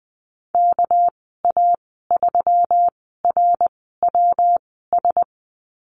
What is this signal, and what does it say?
Morse code, heard as a keyed tone, about 20 wpm, spaced as for 14 wpm, 700 Hz. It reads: KA3RWS
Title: 213 Lesson 34g - Number 8 - Call Signs 20wpm